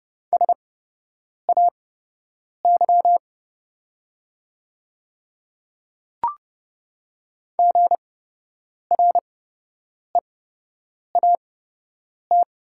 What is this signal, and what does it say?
Morse code, heard as a keyed tone, about 30 wpm, spaced as for 8 wpm, 700 Hz. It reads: SAY GREAT
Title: 8x Character Spacing - Top 300 Words - 30wpm